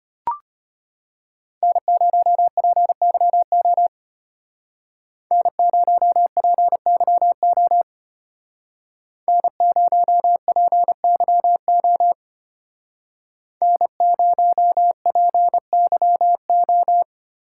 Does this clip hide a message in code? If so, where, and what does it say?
Yes: Morse code from 1.6 to 17.0 s, heard as a keyed tone, about 32 wpm, 700 Hz. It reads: N0PYO N0PYO N0PYO N0PYO